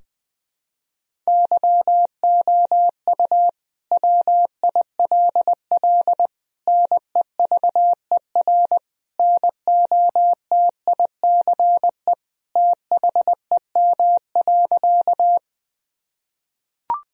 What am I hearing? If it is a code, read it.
Morse code, heard as a keyed tone, about 21 wpm, 700 Hz. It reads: YOU WILL NEVER NOTICE THEM.
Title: Sentences from Top 1000 Words - Review Part 2 - 20wpm